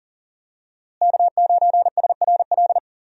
Morse code, heard as a keyed tone, about 39 wpm, 700 Hz. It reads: K9SRL
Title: USA Call Signs - 5 Characters - Repeated 2x - 40wpm